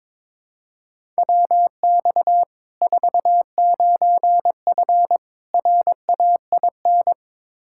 Morse code, heard as a keyed tone, about 22 wpm, 700 Hz. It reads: WX 49F RAIN
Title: Practice QSO - Exchange 2 - Info - 22wpm